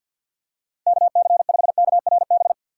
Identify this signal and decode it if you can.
Morse code, heard as a keyed tone, about 50 wpm, 700 Hz. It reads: KC5KRB
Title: USA Call Signs - 3x Character Spacing - 50wpm